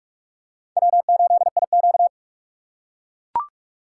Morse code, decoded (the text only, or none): W8IQ